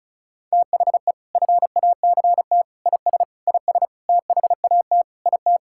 Morse code, heard as a keyed tone, about 34 wpm, 700 Hz. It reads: THE FACT IS IS THAT IT
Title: Sentences from Top 1600 Words 35wpm